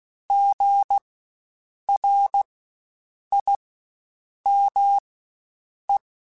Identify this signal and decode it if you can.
Morse code, heard as a keyed tone, about 16 wpm, spaced as for 7 wpm, 800 Hz. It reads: GRIME
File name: Spaced Alphas / Spaced Alphas mp3s / Gs sp @15